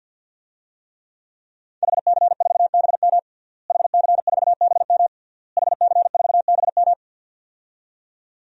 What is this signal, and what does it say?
Morse code, heard as a keyed tone, about 50 wpm, 700 Hz. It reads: HC4BM HC4BM HC4BM
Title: DXCC-100 Call Signs - 5 Characters - Repeated 3x - 50wpm